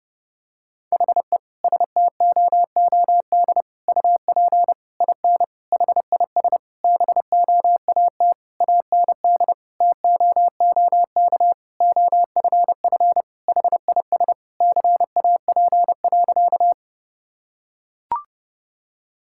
Morse code, heard as a keyed tone, about 30 wpm, 700 Hz. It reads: HE STOOD UP IN HIS BOAT AND TOOK OFF HIS CAP.
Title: Sentences from Top 1600 Words 30wpm